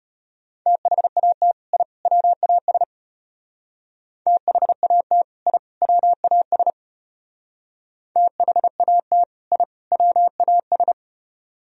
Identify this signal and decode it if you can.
Morse code, heard as a keyed tone, about 30 wpm, 700 Hz. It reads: THAT I WAS THAT I WAS THAT I WAS
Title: Speed-Racing - Sets of 3 Words 25wpm